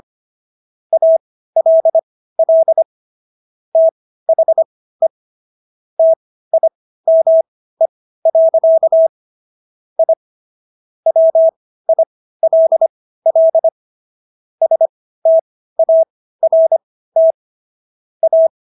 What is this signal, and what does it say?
Morse code, heard as a keyed tone, about 25 wpm, spaced as for 15 wpm, 650 Hz. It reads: ALL THE TIME. I WILL START A